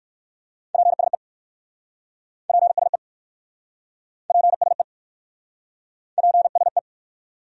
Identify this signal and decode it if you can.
Morse code, heard as a keyed tone, about 59 wpm, 700 Hz. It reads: PSE PSE PSE PSE